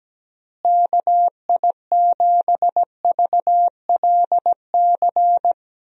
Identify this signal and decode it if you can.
Morse code, heard as a keyed tone, about 17 wpm, 700 Hz. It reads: KI7VLC
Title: USA Call Signs - 6 Characters - Repeated 3x - 17wpm